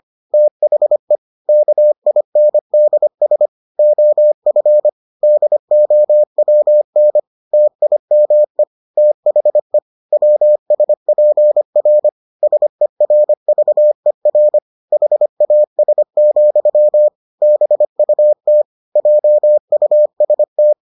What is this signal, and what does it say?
Morse code, heard as a keyed tone, about 25 wpm, 600 Hz. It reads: THE KINDS OF DOWN TIME THE WSPR SERVER HAS, BUT JUST